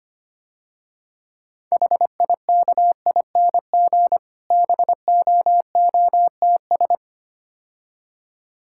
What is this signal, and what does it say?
Morse code, heard as a keyed tone, about 25 wpm, 700 Hz. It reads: HIKING BOOTS
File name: practice / 25wpm / Difficult Words - Speed-Racing with Hints - 25wpm